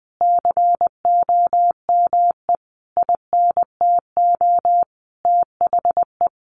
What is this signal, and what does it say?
Morse code, heard as a keyed tone, about 19 wpm, 700 Hz. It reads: COME INTO THE
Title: Sentences from Top 700 Words 20wpm